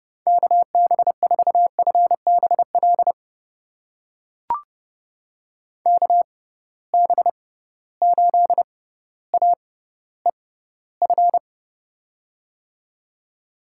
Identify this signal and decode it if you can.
Morse code, heard as a keyed tone, about 30 wpm, spaced as for 15 wpm, 700 Hz. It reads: KB4FBL KB8AEF
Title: USA Call Signs - 6 Characters - 6x Character Spacing - 30wpm